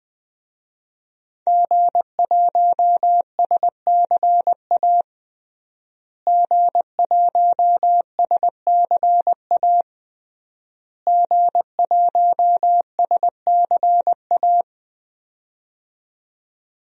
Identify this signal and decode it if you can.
Morse code, heard as a keyed tone, about 21 wpm, 700 Hz. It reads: G1SCA G1SCA G1SCA
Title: DXCC-100 Call Signs - 5 Characters - Repeated 3x - 20wpm